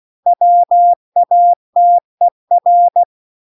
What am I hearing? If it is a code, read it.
Morse code, heard as a keyed tone, about 16 wpm, 700 Hz. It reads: WATER